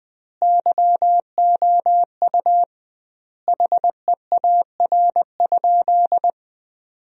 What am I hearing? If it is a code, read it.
Morse code, heard as a keyed tone, about 20 wpm, 700 Hz. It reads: YOU HEAR?
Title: 2x Word-Spacing - Sentences from Top 300 Words - 20wpm